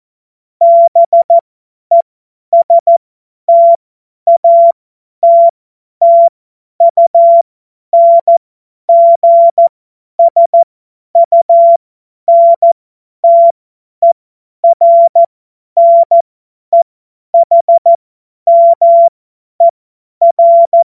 Morse code, heard as a keyed tone, about 14 wpm, spaced as for 10 wpm, 680 Hz. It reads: BESTATTUNGSUNTERNEHMER
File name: Morsekurs / Witze / Witz 14.November 2023 50 F70